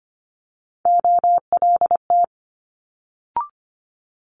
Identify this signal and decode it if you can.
Morse code, heard as a keyed tone, about 25 wpm, 700 Hz. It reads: OLT
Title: Multi-letter Phonemes Quick Fire 25wpm